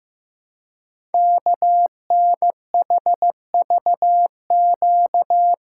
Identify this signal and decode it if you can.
Morse code, heard as a keyed tone, about 15 wpm, 700 Hz. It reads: KNHVQ
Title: Code Groups - 5 Letters - 15wpm